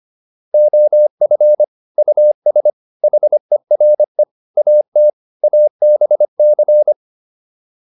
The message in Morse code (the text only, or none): OF US HERE AT ABC